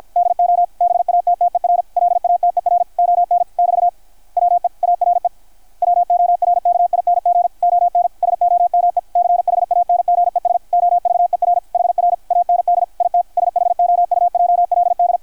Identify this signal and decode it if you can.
Morse code, heard as a keyed tone, about 53 wpm, 690 Hz, with radio noise. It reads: DO BATTER LATER ON <BT> WE ARE WORKING ON SOME CHANGES OVER HR AND IT SHOUJLD
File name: archived / CRSnet